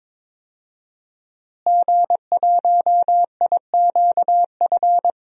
Morse code, heard as a keyed tone, about 22 wpm, 700 Hz. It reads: G1IQF
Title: DXCC-100 Call Signs - 5 Characters - Repeated 2x - 22wpm